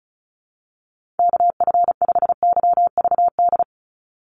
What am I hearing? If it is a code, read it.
Morse code, heard as a keyed tone, about 36 wpm, 700 Hz. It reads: KF5YVD
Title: USA Call Signs 35wpm